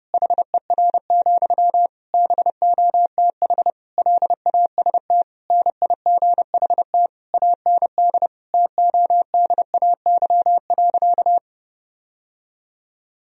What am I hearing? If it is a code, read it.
Morse code, heard as a keyed tone, about 30 wpm, 700 Hz. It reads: HER, BOTH LAST NIGHT AND TODAY.
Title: Sentences from Top 1000 Words - Review Part 4 - 30wpm